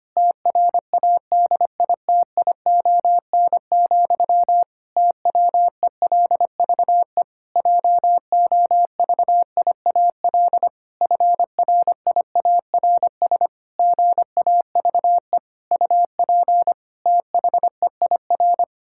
Morse code, heard as a keyed tone, about 25 wpm, 700 Hz. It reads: TRADITION, TWELVE JOVIAL FRIARS GAVE UP THEIR
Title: Pangram Challenge v2 - 25wpm